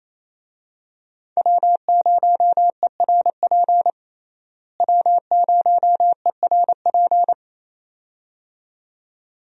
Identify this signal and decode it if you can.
Morse code, heard as a keyed tone, about 28 wpm, 700 Hz. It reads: W0ERP W0ERP
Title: USA Call Signs - 5 Characters - Repeated 2x - 28wpm